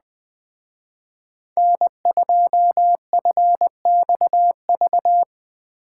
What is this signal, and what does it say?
Morse code, heard as a keyed tone, about 20 wpm, 700 Hz. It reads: N2FXV
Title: Speed Racing - USA 1x3 Call Signs 17wpm